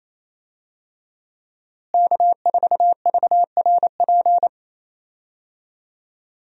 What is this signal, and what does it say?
Morse code, heard as a keyed tone, about 28 wpm, 700 Hz. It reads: K4VRP
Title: USA Call Signs - 5 Characters - Repeated 2x - 28wpm